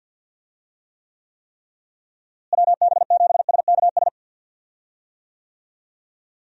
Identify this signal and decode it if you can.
Morse code, heard as a keyed tone, about 46 wpm, 700 Hz. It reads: WB7SKS
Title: USA Call Signs - 6 Characters - Repeated 3x - 50wpm